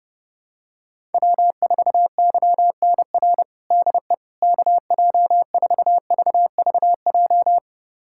Morse code, heard as a keyed tone, about 30 wpm, 700 Hz. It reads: W4YNR DE KJ4VVJ
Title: USA Callsign DE Callsign 30wpm